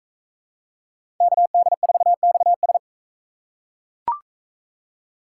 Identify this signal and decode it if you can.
Morse code, heard as a keyed tone, about 40 wpm, 700 Hz. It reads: KD4XS